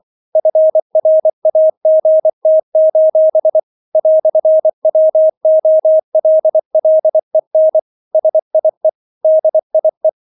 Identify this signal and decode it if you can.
Morse code, heard as a keyed tone, about 24 wpm, 630 Hz. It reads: FRAGT: "WOLLEN SIE DIE